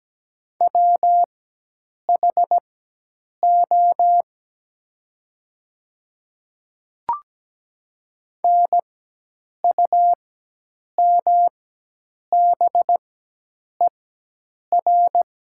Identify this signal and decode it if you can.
Morse code, heard as a keyed tone, about 17 wpm, spaced as for 8 wpm, 700 Hz. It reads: WHO NUMBER
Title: 4x Character Spacing - Top 100 Words - 17wpm